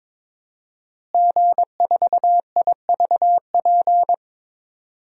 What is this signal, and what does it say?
Morse code, heard as a keyed tone, about 22 wpm, 700 Hz. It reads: G4IVP
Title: DXCC-100 Call Signs - 5 Characters - Repeated 3x - 22wpm